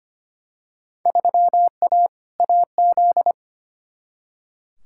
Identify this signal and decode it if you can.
Morse code, heard as a keyed tone, about 25 wpm, 700 Hz. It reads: 3A AZ
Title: Field Day Exchanges - 25wpm